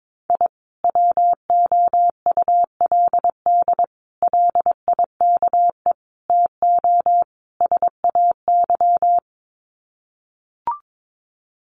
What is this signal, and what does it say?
Morse code, heard as a keyed tone, about 22 wpm, 700 Hz. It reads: I WOULD LIKE TO SAY
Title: Sets of 5 Words Encore2 22wpm